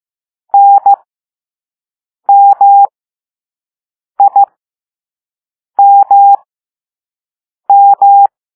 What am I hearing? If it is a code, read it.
Morse code, heard as a keyed tone, about 15 wpm, spaced as for 5 wpm, 800 Hz. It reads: NMIMM